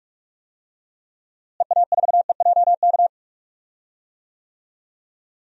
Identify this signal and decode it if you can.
Morse code, heard as a keyed tone, about 44 wpm, 700 Hz. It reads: EA4EJK
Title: DXCC-100 Call Signs - 6 Characters - Repeated 2x - 45wpm